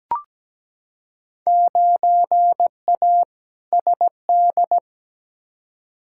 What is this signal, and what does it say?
Morse code, heard as a keyed tone, about 17 wpm, 700 Hz. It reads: 9A SD
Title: Field Day Exchanges Encore - 17wpm